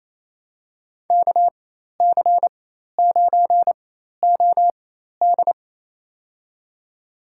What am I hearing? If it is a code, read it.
Morse code, heard as a keyed tone, about 28 wpm, spaced as for 13 wpm, 700 Hz. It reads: KC9OD
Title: USA Call Signs - 4x Character Spacing - 28wpm